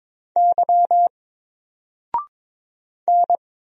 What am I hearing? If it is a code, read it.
Morse code, heard as a keyed tone, about 22 wpm, 700 Hz. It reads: Y N
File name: Single-Letter-Number-22wpm